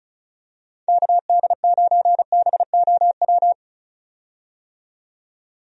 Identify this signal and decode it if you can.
Morse code, heard as a keyed tone, about 36 wpm, 700 Hz. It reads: KD9BOW
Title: USA Call Signs - 6 Characters - 35wpm